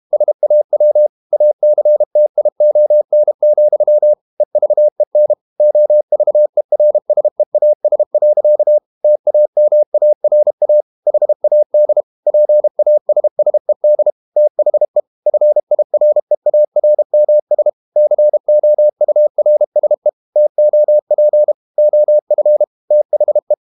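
Morse code, heard as a keyed tone, about 32 wpm, 600 Hz. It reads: SAW ACTION, EVEN OVERSEAS. TAMARA HAD PASSED THE FIREARMS COURSE TOP OF THE
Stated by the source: Morse code practice at 32 wpm from CW med Gnister, reading Ken Follett's novel Never in English.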